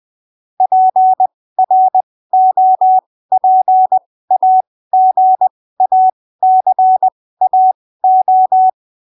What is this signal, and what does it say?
Morse code, heard as a keyed tone, about 20 wpm, spaced as for 15 wpm, 750 Hz. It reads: PROPAGACAO